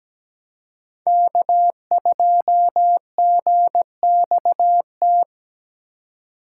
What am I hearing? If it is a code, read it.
Morse code, heard as a keyed tone, about 17 wpm, 700 Hz. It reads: K2GXT
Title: USA Call Signs - 4x Character Spacing - 17wpm